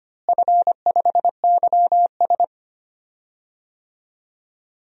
Morse code, heard as a keyed tone, about 25 wpm, 700 Hz. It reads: F5YS
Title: DXCC-100 Call Signs - 4 Characters - Repeated 2x - 25wpm